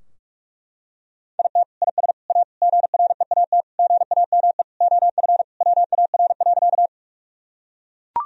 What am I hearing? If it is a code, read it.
Morse code, heard as a keyed tone, about 41 wpm, 700 Hz. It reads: IT IS A GREAT GAME OF WAR.
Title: Sentences from Top 1000 Words - Review Part 3 - 45wpm